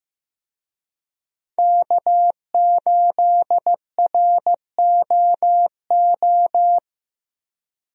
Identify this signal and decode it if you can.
Morse code, heard as a keyed tone, about 15 wpm, 700 Hz. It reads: K8ROO
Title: USA Call Signs - 5 Characters - Repeated 3x - 15wpm